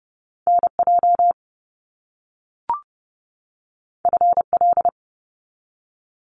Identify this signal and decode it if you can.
Morse code, heard as a keyed tone, about 30 wpm, 700 Hz. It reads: NJ FL
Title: US States 30wpm